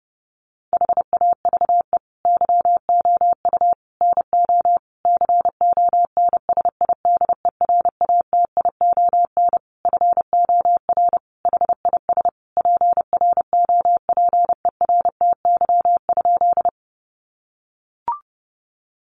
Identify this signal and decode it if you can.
Morse code, heard as a keyed tone, about 28 wpm, 700 Hz. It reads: HAVE YOU NO CONSIDERATION FOR HIS PROPERTY?
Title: Sentences from Top 1500 Words 30wpm